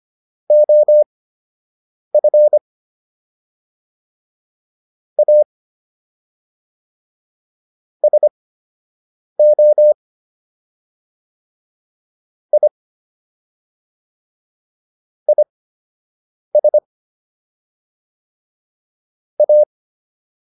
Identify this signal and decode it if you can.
Morse code, heard as a keyed tone, about 25 wpm, spaced as for 7 wpm, 600 Hz. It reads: OF A SO I IS A